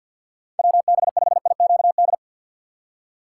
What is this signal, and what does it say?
Morse code, heard as a keyed tone, about 47 wpm, 700 Hz. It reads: WB5IQD
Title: USA Call Signs - 4x Character Spacing - 50wpm